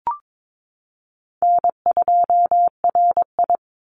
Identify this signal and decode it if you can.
Morse code, heard as a keyed tone, about 22 wpm, 700 Hz. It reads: N2RI
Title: Call Signs Repeated 3x - 22wpm